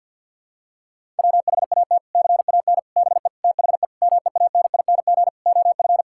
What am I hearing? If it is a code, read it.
Morse code, heard as a keyed tone, about 50 wpm, 700 Hz. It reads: WHAT CAN BE THE MEANING OF
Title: Sentences from Top 1000 Words - Review Part 6 - 50wpm